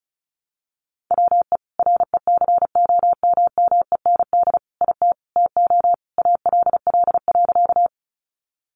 Morse code, heard as a keyed tone, about 34 wpm, 700 Hz. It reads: WE RECOMMEND IT TO ALL.
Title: Sentences from Top 1200 Words 35wpm